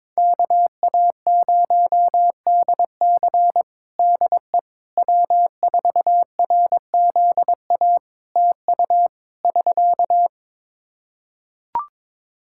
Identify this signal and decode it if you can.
Morse code, heard as a keyed tone, about 23 wpm, 700 Hz. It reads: KA0DC DE W4RZA TU <SK>
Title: Practice QSO - Exchange 4 - Ending - 22wpm